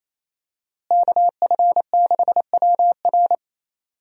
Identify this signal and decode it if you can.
Morse code, heard as a keyed tone, about 28 wpm, 700 Hz. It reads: KF6WR
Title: USA Call Signs - 5 Characters - Repeated 2x - 28wpm